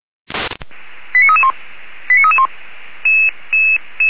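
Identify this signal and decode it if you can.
Five-tone selective call (ZVEI), heard as quick runs of five tones, tones of about 70 ms. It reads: 89331 89331
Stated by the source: ZVEI five-tone call received by SDR, logged by Feuerwehr Orlamünde (fire brigade).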